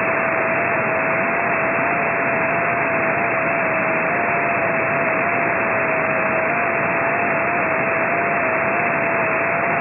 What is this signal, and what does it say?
FT8.